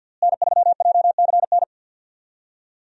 Morse code, heard as a keyed tone, about 48 wpm, 700 Hz. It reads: N3JCN
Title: USA Call Signs - 5 Characters - Repeated 2x - 50wpm